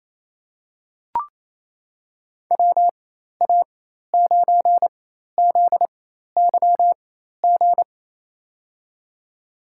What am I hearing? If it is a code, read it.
Morse code, heard as a keyed tone, about 28 wpm, spaced as for 13 wpm, 700 Hz. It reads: WA9ZYG
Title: USA Call Signs - 6 Characters - 4x Character Spacing - 28wpm